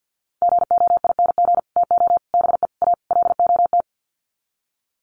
Morse code, heard as a keyed tone, about 50 wpm, 700 Hz. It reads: GOING TO BE A LOT
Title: Sets of 5 Words Encore 50wpm